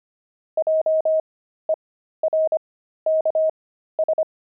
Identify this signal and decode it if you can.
Morse code, heard as a keyed tone, about 25 wpm, spaced as for 13 wpm, 630 Hz. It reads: JERKS
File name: pangram-fs-13-cs-25-jerks